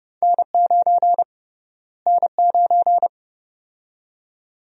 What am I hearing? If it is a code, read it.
Morse code, heard as a keyed tone, about 30 wpm, 700 Hz. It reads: N9 N9
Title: USA Call Sign Prefixes - Repeated 2x - 30wpm